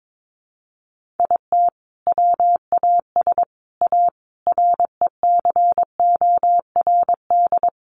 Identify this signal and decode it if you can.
Morse code, heard as a keyed tone, about 22 wpm, 700 Hz. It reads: IT WAS A RECORD
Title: Sentences from Top 500 Words 22wpm